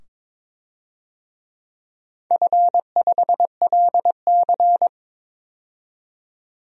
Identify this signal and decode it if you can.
Morse code, heard as a keyed tone, about 22 wpm, 700 Hz. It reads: F5LC
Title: DXCC-100 Call Signs - 4 Characters - 22wpm